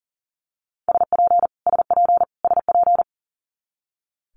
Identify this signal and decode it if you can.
Morse code, heard as a keyed tone, about 40 wpm, 700 Hz. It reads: SP SP SP
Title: Multi-letter Phonemes Repeated 3x 40wpm